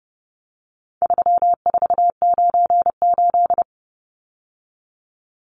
Morse code, heard as a keyed tone, about 30 wpm, 700 Hz. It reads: 3498
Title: Numbers - 4-Digit 30wpm